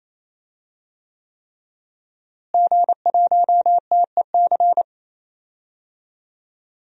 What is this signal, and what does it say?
Morse code, heard as a keyed tone, about 27 wpm, 700 Hz. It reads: G1TEC
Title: DXCC-100 Call Signs - 5 Characters - Repeated 3x - 28wpm